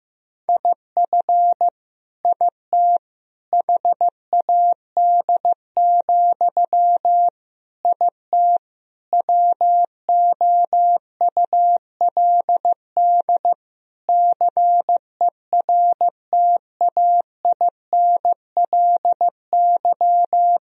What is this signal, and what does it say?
Morse code, heard as a keyed tone, about 15 wpm, 700 Hz. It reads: IF IT HAD, IT WOULD CERTAINLY
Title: Sentences from Top 1000 Words - Review Part 3 - 15wpm